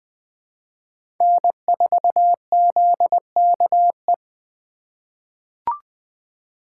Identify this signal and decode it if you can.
Morse code, heard as a keyed tone, about 20 wpm, 700 Hz. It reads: N4ZKE
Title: USA Call Signs - 5 Characters - 4x Character Spacing - 20wpm